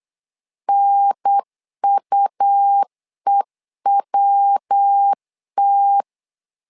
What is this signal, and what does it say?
Morse code, heard as a keyed tone, about 8 wpm, 800 Hz. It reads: NUEWT
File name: LEZIONI / 3 MWRO / 3 MW 8wpm 2501